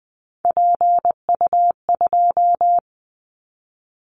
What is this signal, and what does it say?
Morse code, heard as a keyed tone, about 20 wpm, 700 Hz. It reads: PU2
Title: Callsign Prefixes for Top 50 DXCC - 20wpm